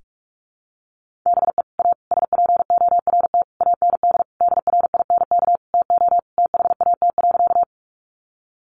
Morse code, heard as a keyed tone, about 44 wpm, 700 Hz. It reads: BE A SPORT AND DRINK TO THAT.
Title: Sentences from Top 1500 Words 45wpm